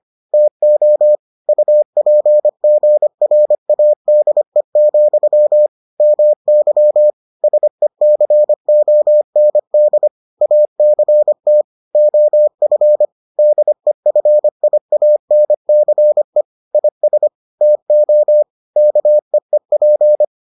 Morse code, heard as a keyed tone, about 25 wpm, 600 Hz. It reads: TO UPGRADE, MY SECOND ACT OF DEFIANCE IS TO KEEP